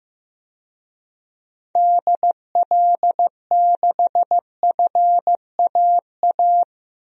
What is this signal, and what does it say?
Morse code, heard as a keyed tone, about 15 wpm, 700 Hz. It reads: DL6FAA
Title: DXCC-100 Call Signs - 6 Characters - Repeated 3x - 15wpm